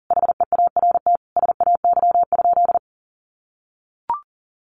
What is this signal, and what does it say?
Morse code, heard as a keyed tone, about 40 wpm, 700 Hz. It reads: HEART SAY?
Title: Sentences from Top 1200 Words 40wpm